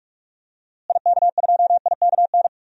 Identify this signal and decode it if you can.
Morse code, heard as a keyed tone, about 46 wpm, 700 Hz. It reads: IK2IKN
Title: DXCC-100 Call Signs - 6 Characters - Repeated 3x - 45wpm